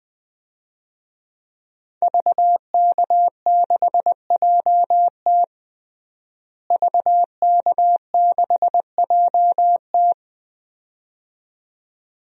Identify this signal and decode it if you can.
Morse code, heard as a keyed tone, about 21 wpm, 700 Hz. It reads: VK6JT VK6JT
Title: DXCC-100 Call Signs - 5 Characters - Repeated 2x - 20wpm